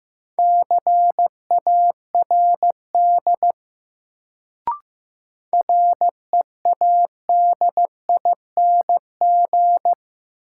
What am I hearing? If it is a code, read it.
Morse code, heard as a keyed tone, about 15 wpm, 700 Hz. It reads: CARD READING
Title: Sentences from Top 600 Words - New Word Drill - 15wpm